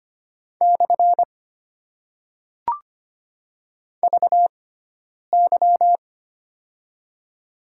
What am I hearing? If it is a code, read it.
Morse code, heard as a keyed tone, about 25 wpm, spaced as for 9 wpm, 690 Hz. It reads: / VY
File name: Top 90 QSO Elements - 6x Character Spacing - 25wpm